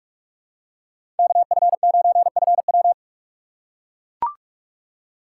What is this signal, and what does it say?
Morse code, heard as a keyed tone, about 42 wpm, 700 Hz. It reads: KF9FW
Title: USA Call Signs - 5 Characters - 6x Character Spacing - 45wpm